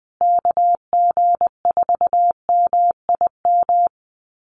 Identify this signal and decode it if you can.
Morse code, heard as a keyed tone, about 20 wpm, 700 Hz. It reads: KG4MIM
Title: USA Call Signs Repeated 3x 20wpm